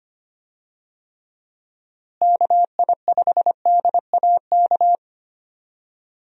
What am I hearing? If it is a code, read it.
Morse code, heard as a keyed tone, about 25 wpm, 700 Hz. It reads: KI5DAK
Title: USA Call Signs - 6 Characters - Repeated 3x - 25wpm